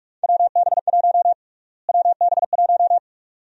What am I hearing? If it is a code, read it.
Morse code, heard as a keyed tone, about 45 wpm, 700 Hz. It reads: WB1 WB1
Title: USA Call Sign Prefixes - Repeated 2x - 45wpm